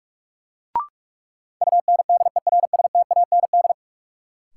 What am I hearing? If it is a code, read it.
Morse code, heard as a keyed tone, about 46 wpm, 700 Hz. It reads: UNDERSTAND
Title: Sentences from Top 600 Words - New Word Drill - 45wpm